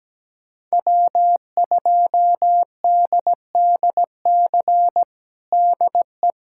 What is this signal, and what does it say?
Morse code, heard as a keyed tone, about 17 wpm, 700 Hz. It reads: W2DDC DE
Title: USA Callsign DE Callsign - 5 Characters - 17wpm